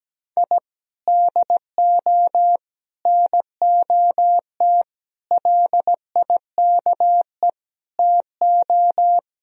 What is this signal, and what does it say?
Morse code, heard as a keyed tone, about 17 wpm, 700 Hz. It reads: I DO NOT LIKE TO
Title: Speed-Racing - Sets of 5 Words 17wpm